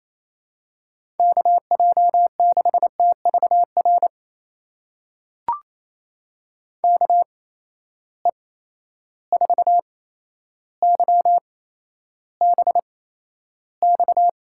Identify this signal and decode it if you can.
Morse code, heard as a keyed tone, about 27 wpm, spaced as for 12 wpm, 700 Hz. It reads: KJ6TVR KE4YBX
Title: USA Call Signs - 6 Characters - 8x Character Spacing - 28wpm